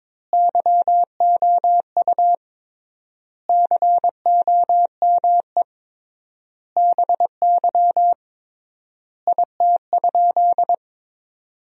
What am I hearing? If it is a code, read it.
Morse code, heard as a keyed tone, about 22 wpm, 700 Hz. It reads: YOU COME BY IT?
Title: 3x Word-Spacing - Sentences from Top 100 Words - 22wpm